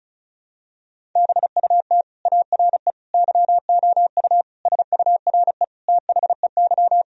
Morse code, heard as a keyed tone, about 36 wpm, 700 Hz. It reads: BUT ARE YOU SURE THEY
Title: Sentences from Top 1000 Words - Review Part 1 - 35wpm